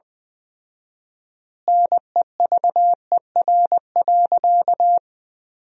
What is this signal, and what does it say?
Morse code, heard as a keyed tone, about 20 wpm, 700 Hz. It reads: NEVER.
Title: 4x Word-Spacing - Sentences from Top 400 Words - 20wpm